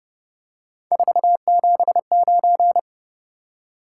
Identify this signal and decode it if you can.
Morse code, heard as a keyed tone, about 30 wpm, 700 Hz. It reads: 479
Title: Top 180 QSO Elements 30wpm